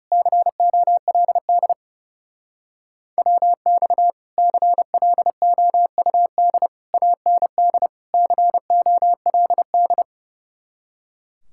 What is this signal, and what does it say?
Morse code, heard as a keyed tone, about 36 wpm, 700 Hz. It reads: COLD WX CLOUD AND COLD
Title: Speed-Racing Common QSO Phrases Encore - 30wpm